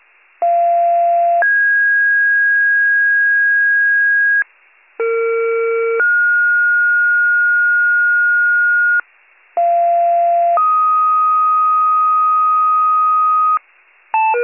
Two-tone page, heard as a one-second tone, then a three-second tone, about 690 then 1650 Hz, about 460 then 1430 Hz, about 690 then 1250 Hz, with noise. It is heard from 0.4 to 13.6 s.